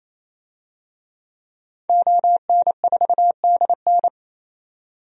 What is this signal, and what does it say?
Morse code, heard as a keyed tone, about 28 wpm, 700 Hz. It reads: ON4DN
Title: DXCC-100 Call Signs - 5 Characters - 28wpm